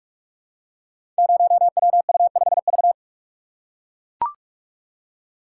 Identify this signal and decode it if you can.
Morse code, heard as a keyed tone, about 44 wpm, 700 Hz. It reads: 0WU5V
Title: Code Groups - 5 Letter:Numbers - 45wpm